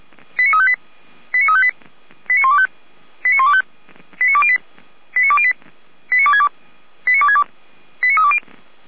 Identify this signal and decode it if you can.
Five-tone selective call (ZVEI), heard as quick runs of five tones, tones of about 70 ms. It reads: 89358 89358 89125 89125 89208 89208 89262 89262 89320
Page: SDR/ZVEI